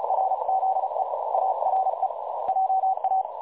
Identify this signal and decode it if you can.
Morse code, heard as a keyed tone, about 36 wpm, 760 Hz, with noise. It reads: RZE ARE JA
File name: dlazaawansowanych / nagraniazpasma / websdr recording 3565.0kHz